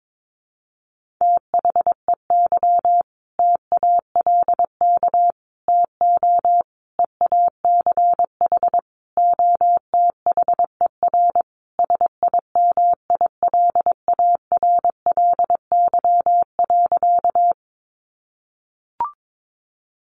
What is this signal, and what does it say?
Morse code, heard as a keyed tone, about 22 wpm, 700 Hz. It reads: THEY TALK TO EACH OTHER SIMILARLY.
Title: Sentences from Top 1400 Words 22wpm